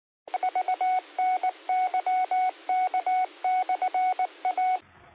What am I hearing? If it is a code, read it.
Morse code, heard as a keyed tone, about 19 wpm, 740 Hz. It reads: VNYK/A
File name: qsorec SimplexLogic